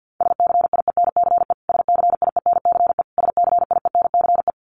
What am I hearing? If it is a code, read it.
Morse code, heard as a keyed tone, about 50 wpm, 700 Hz. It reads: SCIENCE SCIENCE SCIENCE
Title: Top 500 Words Repeated 50wpm